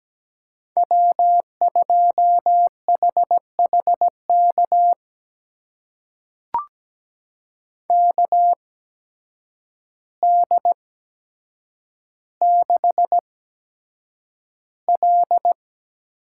Morse code, heard as a keyed tone, about 17 wpm, spaced as for 8 wpm, 700 Hz. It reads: W2HHK KD6L
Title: USA Call Signs - 5 Characters - 8x Character Spacing - 17wpm